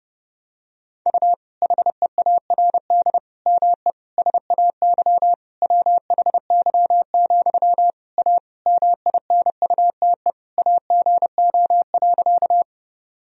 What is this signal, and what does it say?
Morse code, heard as a keyed tone, about 30 wpm, 700 Hz. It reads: U HEARD ME SAY WHY, A MINUTE AGO.
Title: Sentences from Top 500 Words and Abbreviations 30wpm